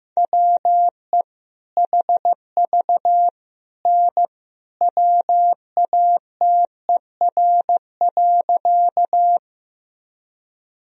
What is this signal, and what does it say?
Morse code, heard as a keyed tone, about 15 wpm, 700 Hz. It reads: WE HV N WATER.
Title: Sentences from Top 100 Words and Abbreviations 15wpm